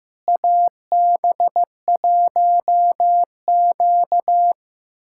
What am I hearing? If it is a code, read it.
Morse code, heard as a keyed tone, about 16 wpm, 700 Hz. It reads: AB1Q
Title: USA Callsign DE Callsign - 4 Characters - 15wpm